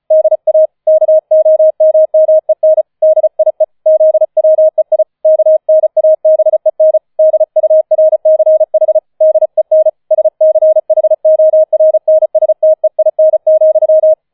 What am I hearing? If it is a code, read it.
Morse code, heard as a keyed tone, about 34 wpm, 630 Hz. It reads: DA KOMMEN DIE ZWEI KNABEN DURCH DEN SCHORNSTEIN,